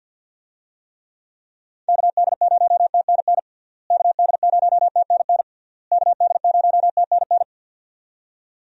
Morse code, heard as a keyed tone, about 52 wpm, 700 Hz. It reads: KD0TNN KD0TNN KD0TNN
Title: USA Call Signs - 6 Characters - Repeated 3x - 50wpm